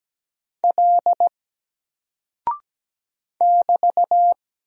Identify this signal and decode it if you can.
Morse code, heard as a keyed tone, about 17 wpm, 700 Hz. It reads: L <BT>
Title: Single Character (Letter:Number:Punctuation:Prosign) 17wpm